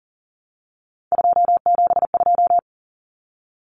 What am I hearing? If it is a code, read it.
Morse code, heard as a keyed tone, about 40 wpm, 700 Hz. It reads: 272
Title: Numbers - 3-Digit 40wpm